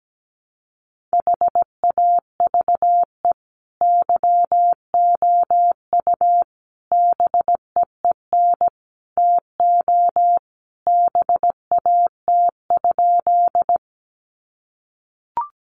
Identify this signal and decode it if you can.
Morse code, heard as a keyed tone, about 17 wpm, 700 Hz. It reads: HAVE YOU BEEN TO BAT?
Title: Sentences from Top 1500 Words 17wpm